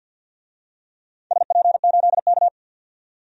Morse code, heard as a keyed tone, about 50 wpm, 700 Hz. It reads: SP8K
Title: DXCC-100 Call Signs - 4 Characters - 50wpm